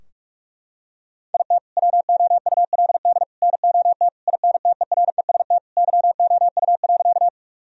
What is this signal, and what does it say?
Morse code, heard as a keyed tone, about 46 wpm, 700 Hz. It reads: IT WOULD NOT INTEREST YOU.
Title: Sentences from Top 1000 Words - Review Part 2 - 45wpm